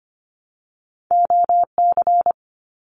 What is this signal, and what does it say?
Morse code, heard as a keyed tone, about 25 wpm, 700 Hz. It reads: OC
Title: Abbreviations Repeated 3x - 25wpm